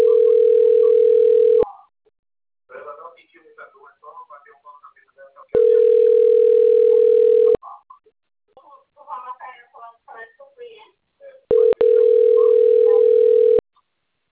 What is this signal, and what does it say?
Ringback tone (North America), 440 + 480 Hz, with speech.
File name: rg-1600-unknown- - - .29078